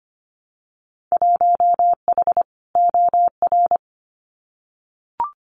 Morse code, heard as a keyed tone, about 25 wpm, 700 Hz. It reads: 1H OR
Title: Winter Field Day Exchanges 25wpm